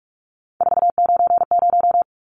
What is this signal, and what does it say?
Morse code, heard as a keyed tone, about 44 wpm, 700 Hz. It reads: 490
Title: Numbers - 3-Digit 45wpm